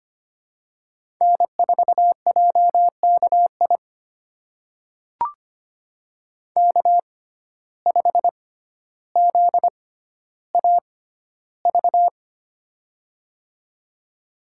Morse code, heard as a keyed tone, about 25 wpm, spaced as for 13 wpm, 690 Hz. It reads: N4JKI K5ZAV
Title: USA Call Signs - 5 Characters - 6x Character Spacing - 25wpm